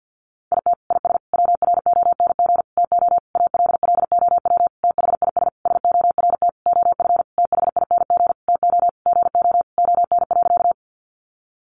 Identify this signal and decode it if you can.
Morse code, heard as a keyed tone, about 50 wpm, 700 Hz. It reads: IT IS WRONG TO ALLOW THIS SORT OF THING TO GO ON.
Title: Sentences from Top 1300 Words 50wpm